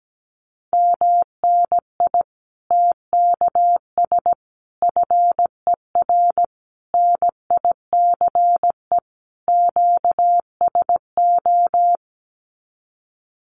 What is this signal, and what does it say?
Morse code, heard as a keyed tone, about 17 wpm, 700 Hz. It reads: MNI TKS FER NICE QSO
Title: Common QSO Phrases 17wpm